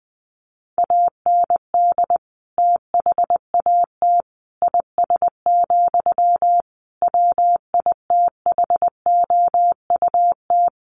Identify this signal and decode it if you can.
Morse code, heard as a keyed tone, about 20 wpm, 700 Hz. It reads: AND THAT IS, WITHOUT
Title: Sentences from Top 1500 Words 20wpm